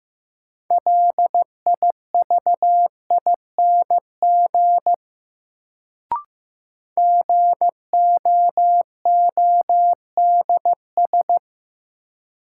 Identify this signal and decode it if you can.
Morse code, heard as a keyed tone, about 15 wpm, 700 Hz. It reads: LIVING GOODS
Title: Sentences from Top 1000 Words - Review Part 6 - New Word Drill - 15wpm